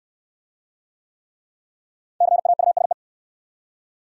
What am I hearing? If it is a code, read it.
Morse code, heard as a keyed tone, about 59 wpm, 700 Hz. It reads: CAUSE